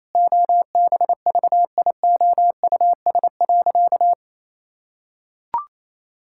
Morse code, heard as a keyed tone, about 28 wpm, 700 Hz. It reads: OBVIOUS.
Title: Sentences from Top 1600 Words 28wpm